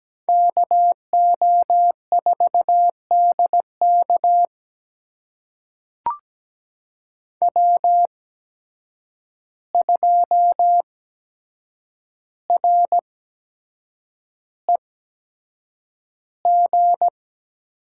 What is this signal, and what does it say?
Morse code, heard as a keyed tone, about 17 wpm, spaced as for 7 wpm, 700 Hz. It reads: KO4DK W2REG
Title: USA Call Signs - 5 Characters - 8x Character Spacing - 17wpm